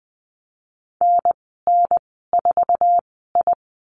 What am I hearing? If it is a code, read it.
Morse code, heard as a keyed tone, about 19 wpm, spaced as for 14 wpm, 700 Hz. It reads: NN4I
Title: 048 Lesson 9g - Number 4 - Call Signs 20wpm